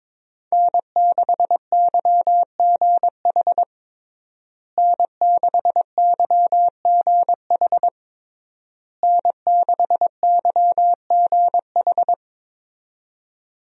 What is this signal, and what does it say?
Morse code, heard as a keyed tone, about 22 wpm, 700 Hz. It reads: N6YGH N6YGH N6YGH
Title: USA Call Signs - 5 Characters - Repeated 3x - 22wpm